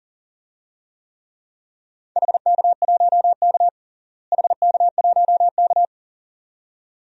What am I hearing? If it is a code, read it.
Morse code, heard as a keyed tone, about 40 wpm, 700 Hz. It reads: HK1K HK1K
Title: DXCC-100 Call Signs - 4 Characters - Repeated 2x - 40wpm